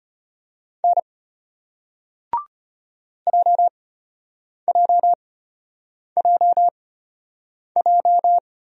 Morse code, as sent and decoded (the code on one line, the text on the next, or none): -. / .--- .--- .--- .---
N JJJJ